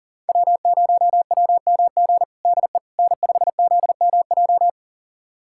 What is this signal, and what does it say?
Morse code, heard as a keyed tone, about 40 wpm, 700 Hz. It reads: W0WMG DE N5ZMJ
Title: USA Callsign DE Callsign - 5 Characters - 40wpm